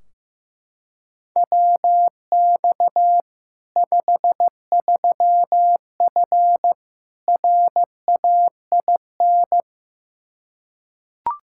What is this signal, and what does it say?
Morse code, heard as a keyed tone, about 15 wpm, 700 Hz. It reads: WX 53F RAIN
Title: Practice QSO - Exchange 2 - Info - 15wpm